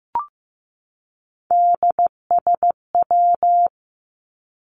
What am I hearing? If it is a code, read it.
Morse code, heard as a keyed tone, about 15 wpm, 700 Hz. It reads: DSW
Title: Abbreviations 15wpm